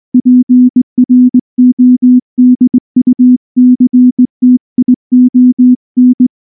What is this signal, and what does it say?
Morse code, heard as a keyed tone, about 18 wpm, 250 Hz. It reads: PRODUCTION
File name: morse-training-20WPM